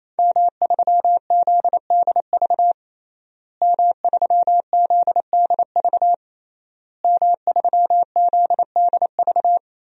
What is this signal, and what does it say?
Morse code, heard as a keyed tone, about 28 wpm, 700 Hz. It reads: M3ZDV M3ZDV M3ZDV
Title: DXCC-100 Call Signs - 5 Characters - Repeated 3x - 28wpm